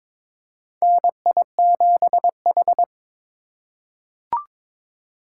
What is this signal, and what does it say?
Morse code, heard as a keyed tone, about 22 wpm, 700 Hz. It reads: NI7H
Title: USA Call Signs - 4 Characters - 2x Character Spacing - 22wpm